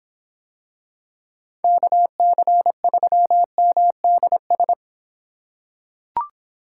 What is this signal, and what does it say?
Morse code, heard as a keyed tone, about 24 wpm, 700 Hz. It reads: KC3MDS